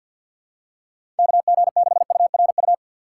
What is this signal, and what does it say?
Morse code, heard as a keyed tone, about 53 wpm, 700 Hz. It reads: KG6URU
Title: USA Call Signs - 6x Character Spacing - 50wpm